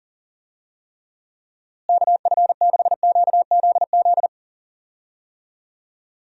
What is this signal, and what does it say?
Morse code, heard as a keyed tone, about 40 wpm, 700 Hz. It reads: KF6QZZ
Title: USA Call Signs - 6 Characters - Repeated 2x - 40wpm